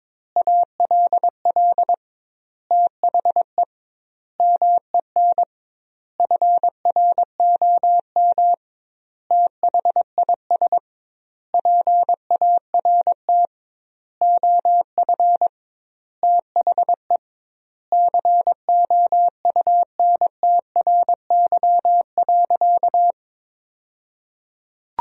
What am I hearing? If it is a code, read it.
Morse code, heard as a keyed tone, about 22 wpm, 700 Hz. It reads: ALL THE MEN FROM THIS PART OF THE COUNTRY.
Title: 2x Word-Spacing - Sentences from Top 300 Words - 22wpm